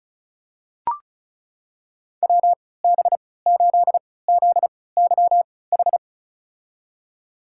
Morse code, heard as a keyed tone, about 36 wpm, spaced as for 20 wpm, 700 Hz. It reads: WB8ZYH
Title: USA Call Signs - 6 Characters - 3x Character Spacing - 35wpm